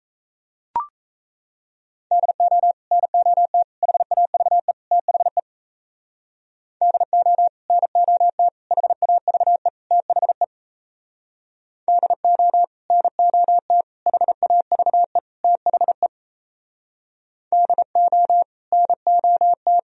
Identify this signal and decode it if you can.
Morse code, heard as a keyed tone, about 34 wpm, 700 Hz. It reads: DO NOT HAVE THE DO NOT HAVE THE DO NOT HAVE THE DO NOT
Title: Speed-Racing - Sets of 4 Words 28wpm